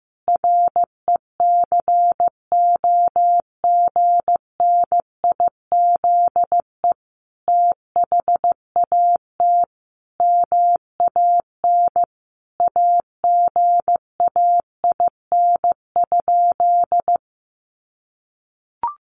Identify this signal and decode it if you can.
Morse code, heard as a keyed tone, about 15 wpm, 700 Hz. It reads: RECOGNIZE THAT MAN AGAIN?
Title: Sentences from Top 1400 Words 15wpm